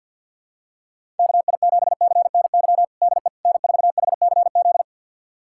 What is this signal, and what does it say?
Morse code, heard as a keyed tone, about 52 wpm, 700 Hz. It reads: KI7CNY DE N4HCZ
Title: USA Callsign DE Callsign 50wpm